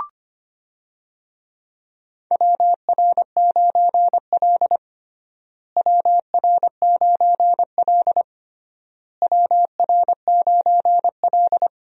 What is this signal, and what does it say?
Morse code, heard as a keyed tone, about 25 wpm, 700 Hz. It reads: WR9L WR9L WR9L
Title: USA Call Signs - 4 Characters - Repeated 3x - 25wpm